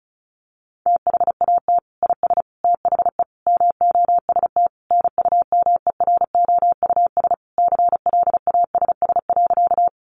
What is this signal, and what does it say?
Morse code, heard as a keyed tone, about 34 wpm, 700 Hz. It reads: THAT IS THE MOST NUMEROUS CLASS.
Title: Sentences from Top 1400 Words 35wpm